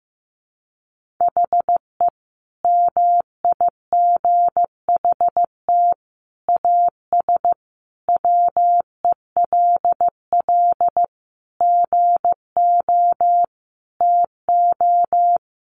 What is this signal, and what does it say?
Morse code, heard as a keyed tone, about 15 wpm, 700 Hz. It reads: HE MIGHT AS WELL GO TO
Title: Sentences from Top 1300 Words 15wpm